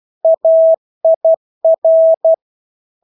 Morse code, heard as a keyed tone, about 12 wpm, 650 Hz. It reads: AIR
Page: Engelske ord 12 wpm